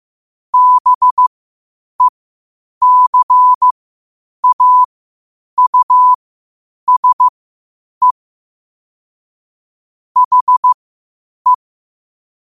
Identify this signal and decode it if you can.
Morse code, heard as a keyed tone, about 15 wpm, spaced as for 8 wpm, 1000 Hz. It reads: BECAUSE HE
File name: qotdfiles / 8wpm